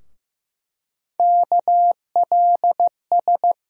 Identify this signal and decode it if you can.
Morse code, heard as a keyed tone, about 15 wpm, 700 Hz. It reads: KLS
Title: USA Call Sign Postfixes - 3 Letters 15wpm